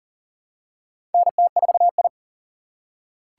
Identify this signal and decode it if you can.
Morse code, heard as a keyed tone, about 40 wpm, 700 Hz. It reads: NT4I
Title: USA Call Signs - 4 Characters - 3x Character Spacing - 40wpm